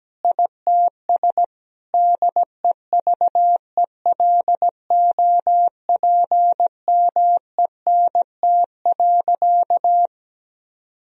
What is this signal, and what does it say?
Morse code, heard as a keyed tone, about 17 wpm, 700 Hz. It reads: ITS DEVELOPMENT.
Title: Sentences from Top 1000 Words - Review Part 4 - 17wpm